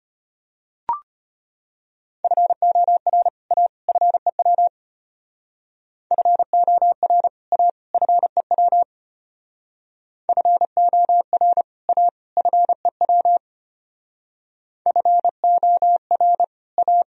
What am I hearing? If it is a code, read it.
Morse code, heard as a keyed tone, about 31 wpm, 700 Hz. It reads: FOR A FEW FOR A FEW FOR A FEW FOR A